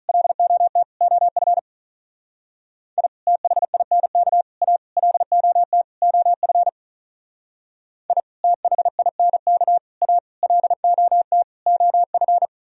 Morse code, heard as a keyed tone, about 45 wpm, 700 Hz. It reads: LOT OF I THINK A LOT OF I THINK A LOT OF